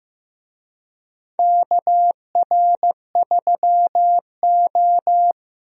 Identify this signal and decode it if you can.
Morse code, heard as a keyed tone, about 15 wpm, 700 Hz. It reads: KR3O
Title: USA Call Signs - 4 Characters - 15wpm